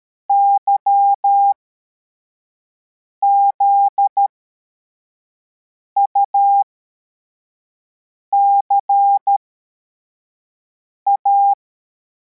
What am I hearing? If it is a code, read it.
Morse code, heard as a keyed tone, about 13 wpm, spaced as for 4 wpm, 800 Hz. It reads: YZUCA
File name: formation / cw / Text53